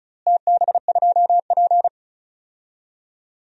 Morse code, heard as a keyed tone, about 34 wpm, 700 Hz. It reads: TB2P
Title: DXCC-100 Call Signs - 4 Characters - Repeated 2x - 35wpm